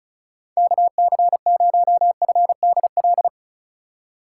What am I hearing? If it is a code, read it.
Morse code, heard as a keyed tone, about 34 wpm, 700 Hz. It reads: KC0FDL
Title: USA Call Signs - 6 Characters - 4x Character Spacing - 35wpm